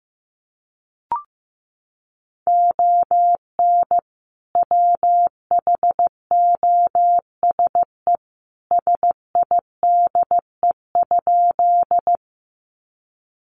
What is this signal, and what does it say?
Morse code, heard as a keyed tone, about 15 wpm, 700 Hz. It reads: ON WHOSE SIDE?
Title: Sentences from Top 1200 Words 15wpm